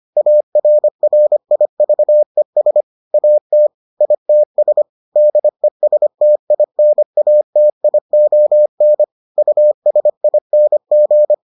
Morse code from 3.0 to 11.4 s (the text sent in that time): AT ITS DESTINATION USING